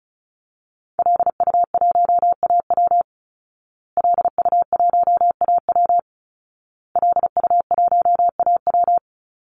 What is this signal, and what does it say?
Morse code, heard as a keyed tone, about 34 wpm, 700 Hz. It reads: LU1AW LU1AW LU1AW
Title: Call Signs Repeated 2x - 35wpm